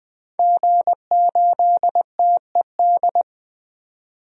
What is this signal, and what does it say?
Morse code, heard as a keyed tone, about 20 wpm, 700 Hz. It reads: G8TED
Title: DXCC-100 Call Signs - 5 Characters - 20wpm